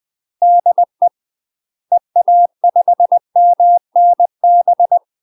Morse code, heard as a keyed tone, about 20 wpm, 700 Hz. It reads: DE EA5MNB
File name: UK MASTER QSOs TRAINING / 20WPM / UK MASTER QSOs 0060